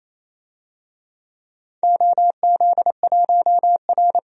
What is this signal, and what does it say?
Morse code, heard as a keyed tone, about 28 wpm, 700 Hz. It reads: OZ1R
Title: DXCC-100 Call Signs - 4 Characters - 28wpm